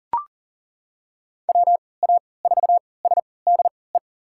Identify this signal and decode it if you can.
Morse code, heard as a keyed tone, about 39 wpm, spaced as for 23 wpm, 700 Hz. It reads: WA4SDE
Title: USA Call Signs - 6 Characters - 3x Character Spacing - 40wpm